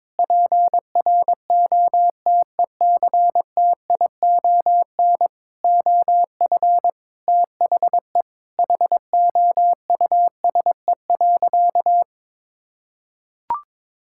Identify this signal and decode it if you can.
Morse code, heard as a keyed tone, about 22 wpm, 700 Hz. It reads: PROTECTION OF THE HOUSE.
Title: Sentences from Top 1600 Words 22wpm